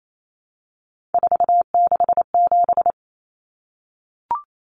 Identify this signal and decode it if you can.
Morse code, heard as a keyed tone, about 28 wpm, 700 Hz. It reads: 467
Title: Common Cut Numbers - 3-Digit 28wpm